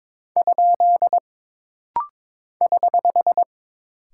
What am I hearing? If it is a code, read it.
Morse code, heard as a keyed tone, about 22 wpm, 700 Hz. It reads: ? <HH>
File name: practice / 22wpm / Single Punctuation-Prosign - Rapid-Fire - 22wpm